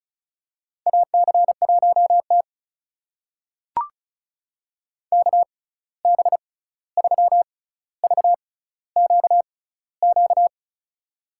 Morse code, heard as a keyed tone, about 34 wpm, spaced as for 16 wpm, 700 Hz. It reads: AC1T KB3VQQ
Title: USA Call Signs - 6x Character Spacing - 35wpm